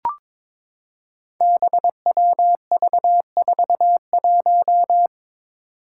Morse code, heard as a keyed tone, about 22 wpm, 700 Hz. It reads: BWV41
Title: Five Challenging Letter-Number Combinations 22wpm